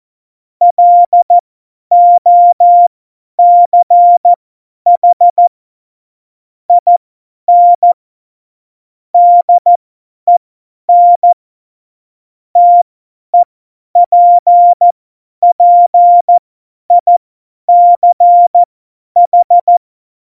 Morse code, heard as a keyed tone, about 14 wpm, spaced as for 10 wpm, 700 Hz. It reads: LOCH IN DEN TEPPICH
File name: Morsekurs / Witze / Witz 01. August 17, Tempo 50 F70